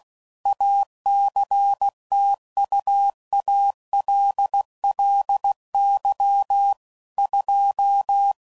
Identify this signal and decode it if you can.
Morse code, heard as a keyed tone, about 16 wpm, 800 Hz. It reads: ACTUALLY 2
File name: CW Code Groups @15WPM / Technical Subjects / PwrSupply